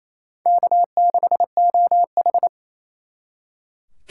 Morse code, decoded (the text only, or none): K6OH